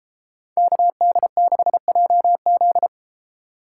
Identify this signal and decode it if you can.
Morse code, heard as a keyed tone, about 33 wpm, 700 Hz. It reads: KD6JZ